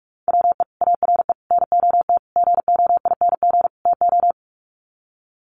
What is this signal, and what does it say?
Morse code, heard as a keyed tone, about 44 wpm, 700 Hz. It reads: WE ARE NOT GOING TO
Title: Sets of 5 Words 45wpm